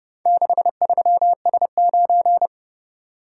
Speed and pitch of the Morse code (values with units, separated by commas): 29 wpm, 700 Hz